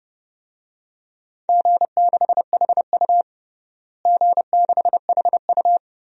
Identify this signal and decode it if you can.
Morse code, heard as a keyed tone, about 30 wpm, 700 Hz. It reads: G6HU G6HU
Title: DXCC-100 Call Signs - 4 Characters - Repeated 3x - 30wpm